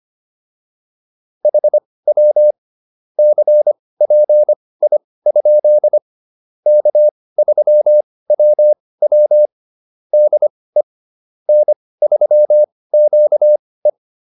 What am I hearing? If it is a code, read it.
Morse code, heard as a keyed tone, about 25 wpm, spaced as for 18 wpm, 600 Hz. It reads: HW CPI? K3WW DE N3QE